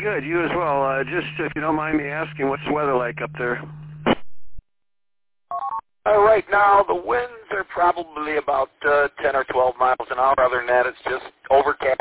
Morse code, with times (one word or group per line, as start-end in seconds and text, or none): none